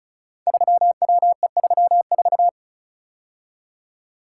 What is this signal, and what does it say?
Morse code, heard as a keyed tone, about 36 wpm, 700 Hz. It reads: 3WE34
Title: Code Groups - 5 Letter:Numbers - 35wpm